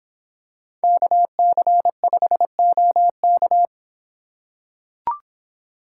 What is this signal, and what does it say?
Morse code, heard as a keyed tone, about 25 wpm, 700 Hz. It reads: KC5OK